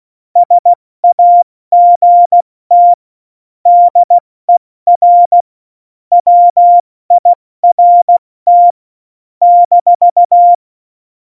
Morse code, heard as a keyed tone, about 16 wpm, 700 Hz. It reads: SAGT DER WIRT -
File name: Morsekurs / Witze / Witz 15. Oktober 2019, 70 F80